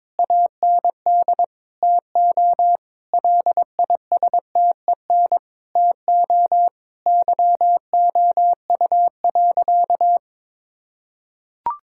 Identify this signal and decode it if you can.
Morse code, heard as a keyed tone, about 22 wpm, 700 Hz. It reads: AND TO LISTEN TO YOU.
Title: Sentences from Top 1600 Words 22wpm